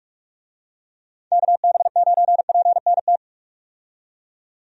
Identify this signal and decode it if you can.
Morse code, heard as a keyed tone, about 41 wpm, 700 Hz. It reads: KB9PNT
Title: USA Call Signs - 6 Characters - 45wpm